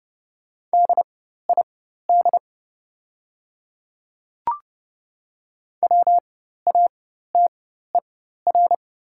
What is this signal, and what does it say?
Morse code, heard as a keyed tone, about 30 wpm, spaced as for 14 wpm, 700 Hz. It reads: DID WATER
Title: 4x Character Spacing - Top 100 Words - 30wpm